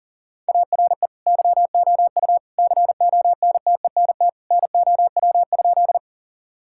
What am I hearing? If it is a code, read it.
Morse code, heard as a keyed tone, about 40 wpm, 700 Hz. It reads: ARE YOU CONTENT NOW?
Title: Sentences from Top 1000 Words - Review Part 6 - 40wpm